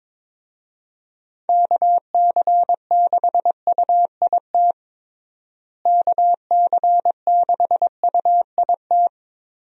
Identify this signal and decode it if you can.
Morse code, heard as a keyed tone, about 22 wpm, 700 Hz. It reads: KC6UIT KC6UIT
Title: USA Call Signs - 6 Characters - Repeated 2x - 22wpm